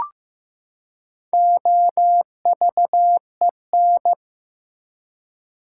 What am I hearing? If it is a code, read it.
Morse code, heard as a keyed tone, about 15 wpm, 700 Hz. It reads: OVEN
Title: Four Letter Words - 15wpm